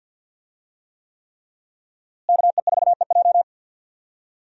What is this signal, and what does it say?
Morse code, heard as a keyed tone, about 52 wpm, 700 Hz. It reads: KE4EJ
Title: USA Call Signs - 5 Characters - Repeated 3x - 50wpm